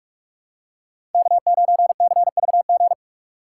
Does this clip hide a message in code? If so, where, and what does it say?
Yes: Morse code from 1.1 to 2.9 s, heard as a keyed tone, about 46 wpm, 700 Hz. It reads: K9CVG